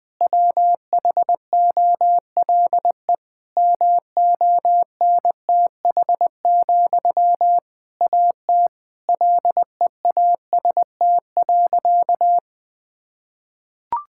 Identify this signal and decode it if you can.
Morse code, heard as a keyed tone, about 20 wpm, 700 Hz. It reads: WHOLE MONTH, AT LEAST.
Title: Sentences from Top 1000 Words - Review Part 5 - 20wpm